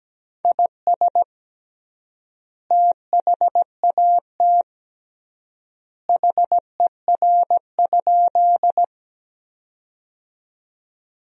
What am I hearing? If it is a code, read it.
Morse code, heard as a keyed tone, about 17 wpm, 700 Hz. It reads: IS THAT HER?
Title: 3x Word-Spacing - Sentences from Top 100 Words - 17wpm